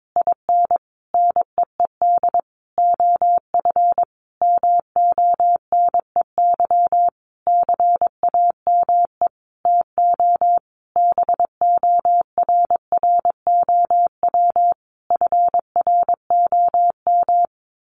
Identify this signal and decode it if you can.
Morse code, heard as a keyed tone, about 22 wpm, 700 Hz. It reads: IN NEED OF MONEY CAME TO BORROW FROM
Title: Sentences from Top 1500 Words 22wpm